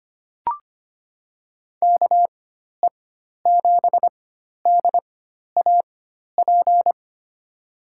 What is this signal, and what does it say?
Morse code, heard as a keyed tone, about 25 wpm, spaced as for 12 wpm, 700 Hz. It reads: KE7DAP
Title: USA Call Signs - 6 Characters - 4x Character Spacing - 25wpm